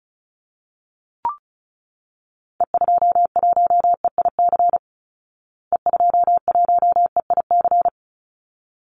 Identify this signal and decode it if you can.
Morse code, heard as a keyed tone, about 34 wpm, 700 Hz. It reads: E21EIC E21EIC
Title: Call Signs Repeated 3x - 35wpm